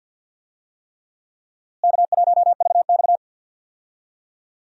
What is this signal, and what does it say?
Morse code, heard as a keyed tone, about 46 wpm, 700 Hz. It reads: K1VX
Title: USA Call Signs - 4 Characters - Repeated 3x - 50wpm